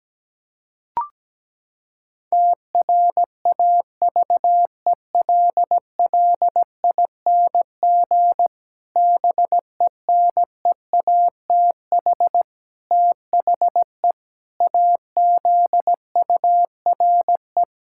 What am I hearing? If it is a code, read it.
Morse code, heard as a keyed tone, about 17 wpm, 700 Hz. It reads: TRAVELLING BENEATH THE AZURE
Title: Pangram Challenge v2 - 17wpm